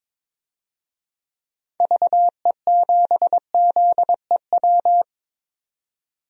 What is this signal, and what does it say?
Morse code, heard as a keyed tone, about 22 wpm, 700 Hz. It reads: VE7ZEW
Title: DXCC-100 Call Signs - 6 Characters - 22wpm